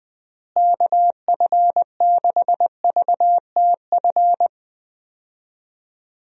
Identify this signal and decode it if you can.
Morse code, heard as a keyed tone, about 20 wpm, 700 Hz. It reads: KF6VTF
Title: USA Call Signs - 6 Characters - Repeated 3x - 20wpm